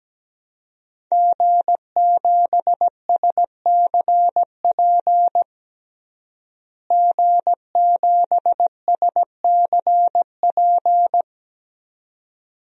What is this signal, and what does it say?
Morse code, heard as a keyed tone, about 17 wpm, 700 Hz. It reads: G7SCP G7SCP
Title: DXCC-100 Call Signs - 5 Characters - Repeated 2x - 17wpm